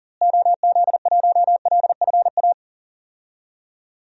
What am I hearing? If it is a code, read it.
Morse code, heard as a keyed tone, about 40 wpm, 700 Hz. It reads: OZ1LFA
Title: DXCC-100 Call Signs - 6 Characters - Repeated 3x - 40wpm